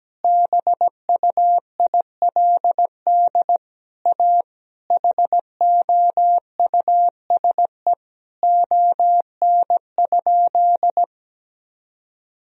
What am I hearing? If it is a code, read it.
Morse code, heard as a keyed tone, about 17 wpm, 700 Hz. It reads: BUILD A HOUSE ON?
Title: Sentences from Top 1000 Words - Review Part 4 - 17wpm